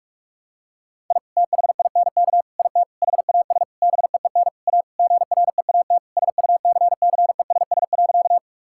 Morse code, heard as a keyed tone, about 46 wpm, 700 Hz. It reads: I THINK IT HAS BEEN A GREAT SUCCESS.
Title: Sentences from Top 1000 Words - Review Part 6 - 45wpm